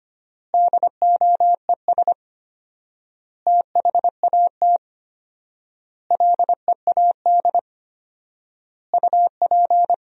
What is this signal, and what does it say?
Morse code, heard as a keyed tone, about 25 wpm, 700 Hz. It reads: DOES THAT LEAD UP
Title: 4x Word-Spacing - Sentences from Top 500 Words - 25wpm